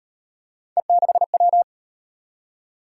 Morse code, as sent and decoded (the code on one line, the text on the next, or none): . -.... .--
E6W